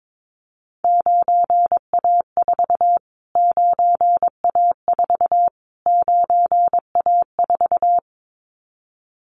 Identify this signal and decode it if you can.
Morse code, heard as a keyed tone, about 22 wpm, 700 Hz. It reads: 9A4 9A4 9A4
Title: Callsign Prefixes for Top 25 DXCC - Repeated 3x - 22wpm